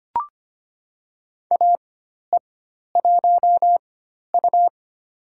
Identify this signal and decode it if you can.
Morse code, heard as a keyed tone, about 25 wpm, spaced as for 12 wpm, 700 Hz. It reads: AE1U
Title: USA Call Signs - 4x Character Spacing - 25wpm